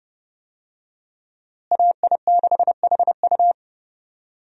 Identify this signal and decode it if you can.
Morse code, heard as a keyed tone, about 31 wpm, 700 Hz. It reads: AI6HU
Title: USA Call Signs - 5 Characters - Repeated 2x - 30wpm